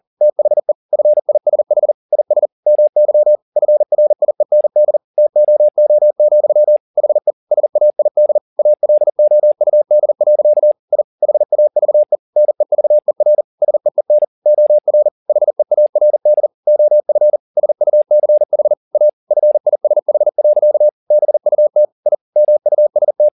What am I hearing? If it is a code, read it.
Morse code, heard as a keyed tone, about 40 wpm, 600 Hz. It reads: THE FISH IS MY FRIEND TOO, HE SAID ALOUD. I HAVE NEVER SEEN OR HEARD OF SUCH A FISH. BUT I MUST
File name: the old man and the sea 2 40wpm